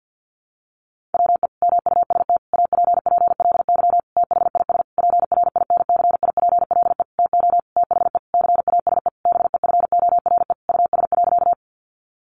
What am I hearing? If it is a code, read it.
Morse code, heard as a keyed tone, about 50 wpm, 700 Hz. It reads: WE MUST APPLY THIS PRINCIPLE TO THE CASE BEFORE US.
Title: Sentences from Top 1400 Words 50wpm